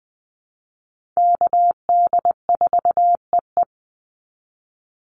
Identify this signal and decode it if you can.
Morse code, heard as a keyed tone, about 20 wpm, 700 Hz. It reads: KD4EE
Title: Call Signs Repeated 3x - 20wpm